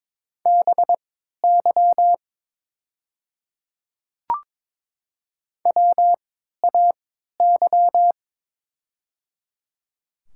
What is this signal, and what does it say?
Morse code, heard as a keyed tone, about 22 wpm, spaced as for 12 wpm, 700 Hz. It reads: BY WAY
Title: 3x Character Spacing - Top 200 Words - 22wpm